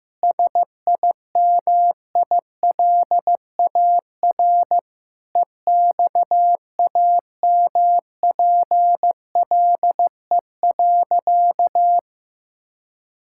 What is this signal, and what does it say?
Morse code, heard as a keyed tone, about 15 wpm, 700 Hz. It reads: SIMILAR EXAMPLE.
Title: Sentences from Top 1000 Words - Review Part 5 - 15wpm